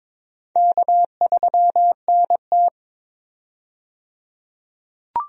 Morse code, heard as a keyed tone, about 23 wpm, 700 Hz. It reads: K3NT
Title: USA Call Signs - 4 Characters - 22wpm